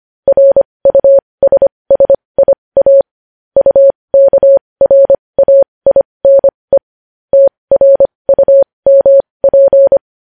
Morse code, heard as a keyed tone, about 25 wpm, spaced as for 20 wpm, 550 Hz. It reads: RUSSIA UKRAINE TRUMP
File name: podcast / News headlines-1x-25x20